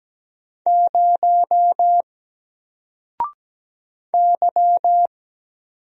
Morse code, heard as a keyed tone, about 17 wpm, 700 Hz. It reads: 0 Y
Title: Single Character (Letter:Number:Punctuation:Prosign) 17wpm